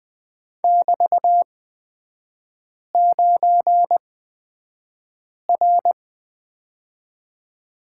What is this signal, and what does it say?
Morse code, heard as a keyed tone, about 20 wpm, spaced as for 5 wpm, 700 Hz. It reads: <BT>9R
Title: 224 Lesson 36e - BT - Three Characters 20wpm